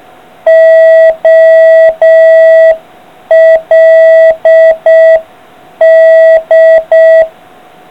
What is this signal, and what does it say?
Morse code, heard as a keyed tone, about 6 wpm, 650 Hz. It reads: OLD